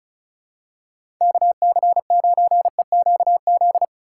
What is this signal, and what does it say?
Morse code, heard as a keyed tone, about 36 wpm, 700 Hz. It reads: KC9EQZ
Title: USA Call Signs - 6 Characters - 3x Character Spacing - 35wpm